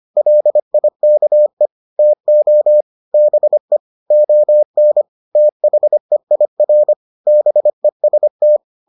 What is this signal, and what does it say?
Morse code, heard as a keyed tone, about 25 wpm, 600 Hz. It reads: LIKE TO BE ON THEIR BEST